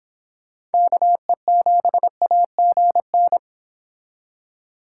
Morse code, heard as a keyed tone, about 29 wpm, 700 Hz. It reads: KE7AGN